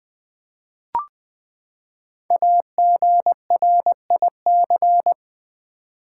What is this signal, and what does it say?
Morse code, heard as a keyed tone, about 20 wpm, 700 Hz. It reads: AGRIC
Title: Build A Word - 5 Letters At A Time - 12-Letter Words - 20wpm